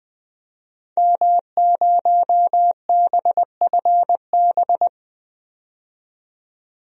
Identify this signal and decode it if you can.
Morse code, heard as a keyed tone, about 20 wpm, 700 Hz. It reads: M0BFB
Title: DXCC-100 Call Signs - 5 Characters - 20wpm